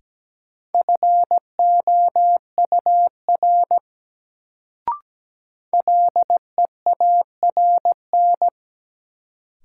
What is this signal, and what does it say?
Morse code, heard as a keyed tone, about 17 wpm, 700 Hz. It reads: FOUR LEARN
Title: Sentences from Top 300 Words - New Word Drill - 17wpm